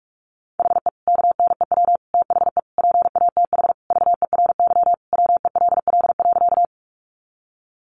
Morse code, heard as a keyed tone, about 44 wpm, 700 Hz. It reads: HE KNEW THE PATH VERY WELL.
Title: Sentences from Top 1400 Words 45wpm